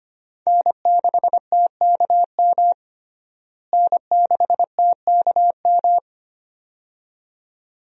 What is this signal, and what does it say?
Morse code, heard as a keyed tone, about 25 wpm, 700 Hz. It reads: N6TKM N6TKM
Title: USA Call Signs - 5 Characters - Repeated 2x - 25wpm